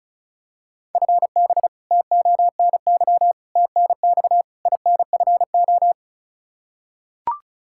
Morse code, heard as a keyed tone, about 36 wpm, 700 Hz. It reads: FB TONY TNX INFO
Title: Practice QSO - Exchange 3 - More Info - 35wpm